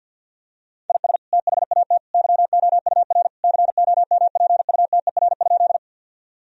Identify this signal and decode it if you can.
Morse code, heard as a keyed tone, about 50 wpm, 700 Hz. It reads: IS THAT YOUR COMPUTER?
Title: Sentences from Top 1000 Words - Review Part 3 - 50wpm